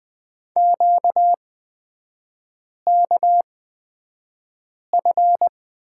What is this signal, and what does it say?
Morse code, heard as a keyed tone, about 20 wpm, spaced as for 5 wpm, 700 Hz. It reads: QKF
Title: 236 Lesson 38e - Letter q - Three Characters 20wpm